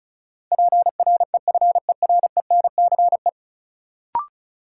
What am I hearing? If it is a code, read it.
Morse code, heard as a keyed tone, about 36 wpm, 700 Hz. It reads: PREFERENCE
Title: Sentences from Top 1500 Words - New Word Drill - 35wpm